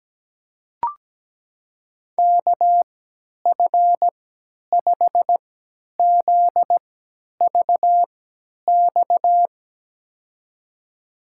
Morse code, heard as a keyed tone, about 17 wpm, spaced as for 10 wpm, 700 Hz. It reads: KF5ZVX
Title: USA Call Signs - 6 Characters - 3x Character Spacing - 17wpm